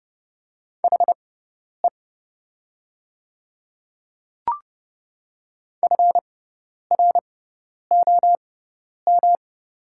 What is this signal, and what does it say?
Morse code, heard as a keyed tone, about 30 wpm, spaced as for 10 wpm, 700 Hz. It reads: HE FROM
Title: 6x Character Spacing - Top 100 Words - 30wpm